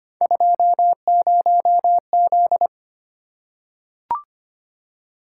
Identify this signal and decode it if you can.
Morse code, heard as a keyed tone, about 25 wpm, 700 Hz. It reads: 20Z
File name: Three Letter-Number Combinations - 25wpm